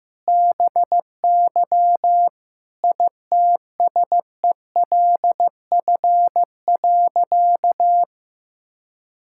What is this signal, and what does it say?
Morse code, heard as a keyed tone, about 15 wpm, 700 Hz. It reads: BY ITSELF.
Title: Sentences from Top 1000 Words - Review Part 5 - 15wpm